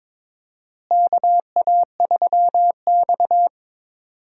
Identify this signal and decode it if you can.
Morse code, heard as a keyed tone, about 22 wpm, 700 Hz. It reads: KA3X
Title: USA Call Signs - 4 Characters - Repeated 3x - 22wpm